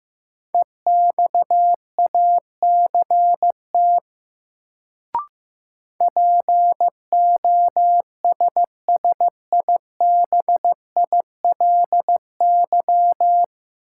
Morse code, heard as a keyed tone, about 15 wpm, 700 Hz. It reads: EXACT POSSIBILY
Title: Sentences from Top 1300 Words - New Word Drill Alt - 15wpm